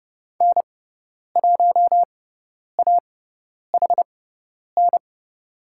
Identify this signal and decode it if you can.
Morse code, heard as a keyed tone, about 30 wpm, spaced as for 10 wpm, 700 Hz. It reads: N1AHN
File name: Lesson 16g - Letter c - Call Signs 30wpm